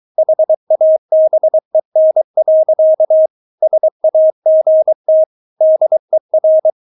Morse code, heard as a keyed tone, about 23 wpm, 630 Hz. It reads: HABEN. SAGT DER
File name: Witz4 23wpm